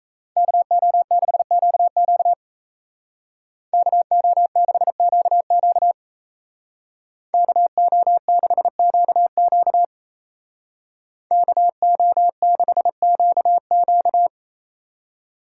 Morse code, heard as a keyed tone, about 36 wpm, 700 Hz. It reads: KO6QQ KO6QQ KO6QQ KO6QQ